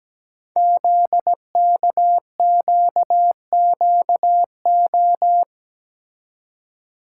Morse code, heard as a keyed tone, about 17 wpm, 700 Hz. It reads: ZKQQO
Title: Code Groups - 5 Letters - 17wpm